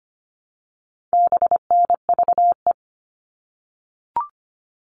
Morse code, heard as a keyed tone, about 25 wpm, 700 Hz. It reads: BNVE
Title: All Cut Numbers - 4-Digit 25wpm